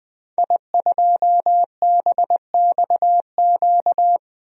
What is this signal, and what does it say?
Morse code, heard as a keyed tone, about 20 wpm, 700 Hz. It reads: I2BXQ
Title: DXCC-100 Call Signs - 5 Characters - Repeated 3x - 20wpm